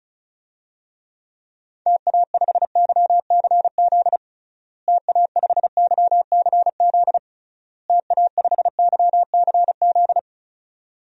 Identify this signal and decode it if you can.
Morse code, heard as a keyed tone, about 36 wpm, 700 Hz. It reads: TA5YCZ TA5YCZ TA5YCZ
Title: DXCC-100 Call Signs - 6 Characters - Repeated 3x - 35wpm